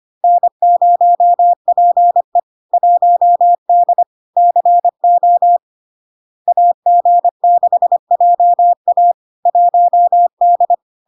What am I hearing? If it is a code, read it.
Morse code, heard as a keyed tone, about 25 wpm, 700 Hz. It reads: N0PE 1D CO AG6JA 1D
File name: Log-in-CW-25wpm-700-Hz